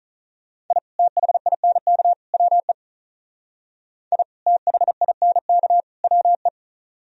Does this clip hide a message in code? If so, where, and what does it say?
Yes: Morse code from 0.7 to 6.5 s, heard as a keyed tone, about 44 wpm, 700 Hz. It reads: I THINK WE I THINK WE